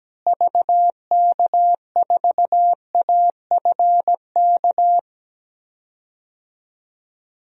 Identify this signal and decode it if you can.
Morse code, heard as a keyed tone, about 17 wpm, 700 Hz. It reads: VK4AFK
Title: DXCC-100 Call Signs - 6 Characters - Repeated 3x - 17wpm